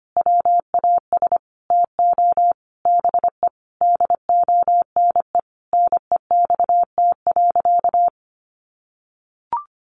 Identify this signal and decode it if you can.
Morse code, heard as a keyed tone, about 25 wpm, 700 Hz. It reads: WAS TO BE DONE NEXT.
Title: Sentences from Top 1200 Words 25wpm